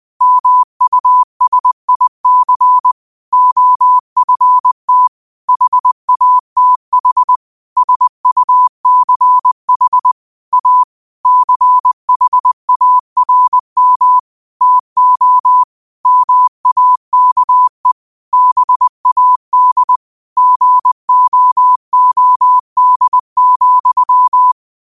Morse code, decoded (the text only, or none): MUSIC OFT HATH SUCH A CHARM TO MAKE BAD GOOD,